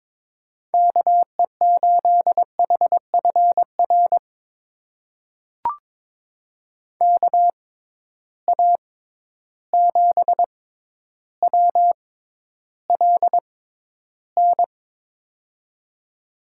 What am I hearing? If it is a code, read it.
Morse code, heard as a keyed tone, about 22 wpm, spaced as for 11 wpm, 700 Hz. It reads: KE8HFR KA7WLN
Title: USA Call Signs - 6 Characters - 6x Character Spacing - 22wpm